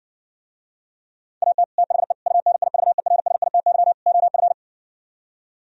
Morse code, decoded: AT THE UNIVERSITY OF

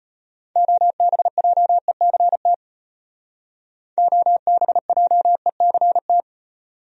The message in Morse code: OBJECT OBJECT

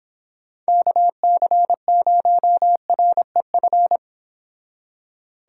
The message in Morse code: KC0REF

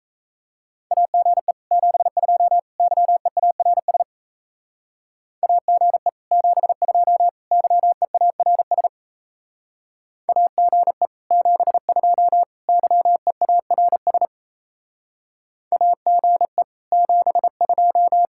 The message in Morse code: AGE 72 YEARS AGE 72 YEARS AGE 72 YEARS AGE 72

